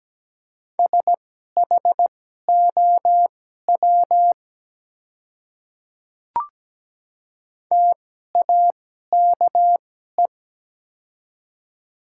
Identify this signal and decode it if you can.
Morse code, heard as a keyed tone, about 17 wpm, spaced as for 12 wpm, 700 Hz. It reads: SHOW TAKE
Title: 2x Character Spacing - Top 200 Words - 17wpm